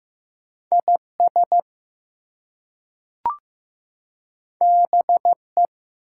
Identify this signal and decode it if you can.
Morse code, heard as a keyed tone, about 15 wpm, 700 Hz. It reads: IS BE
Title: Two Letter Words - 15wpm